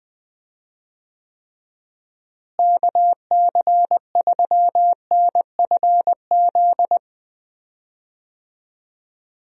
Morse code, heard as a keyed tone, about 20 wpm, 700 Hz. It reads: KC3NFZ
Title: USA Call Signs - 6 Characters - Repeated 3x - 20wpm